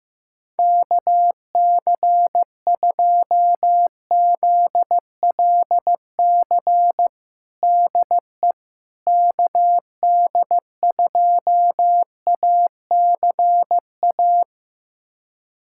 Morse code, heard as a keyed tone, about 15 wpm, 700 Hz. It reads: KC2ZLC DE KD2ACA
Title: USA Callsign DE Callsign 15wpm